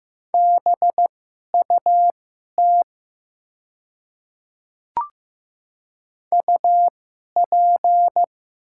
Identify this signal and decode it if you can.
Morse code, heard as a keyed tone, about 15 wpm, spaced as for 11 wpm, 700 Hz. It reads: BUT UP
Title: 2x Character Spacing - Top 100 Words - 15wpm